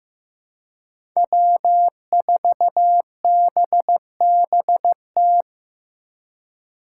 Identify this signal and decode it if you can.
Morse code, heard as a keyed tone, about 16 wpm, 700 Hz. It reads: W4BBT
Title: K1USN SST QSO Practice - 15wpm